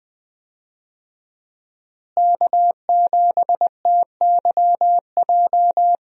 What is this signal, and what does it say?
Morse code, heard as a keyed tone, about 20 wpm, 700 Hz. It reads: K7TYJ
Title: USA Call Signs - 5 Characters - Repeated 2x - 20wpm